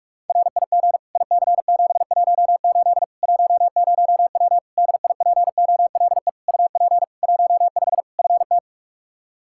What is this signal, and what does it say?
Morse code, heard as a keyed tone, about 42 wpm, 700 Hz. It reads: RIG IC718 10W DIPOLE UP 15 FT
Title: Practice QSO - Exchange 2 - Info - 45wpm